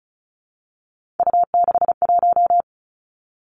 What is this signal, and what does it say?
Morse code, heard as a keyed tone, about 35 wpm, 700 Hz. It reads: U61
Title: Three Challenging Letter-Number Combinations 35wpm